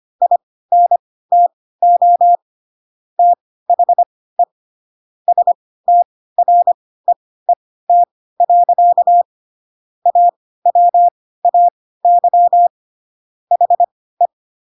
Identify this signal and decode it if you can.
Morse code, heard as a keyed tone, about 25 wpm, spaced as for 16 wpm, 700 Hz. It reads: INTO THE STREET. AWAY HE
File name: Pinocchio / Pinocchio ch3 10 25 ik1hge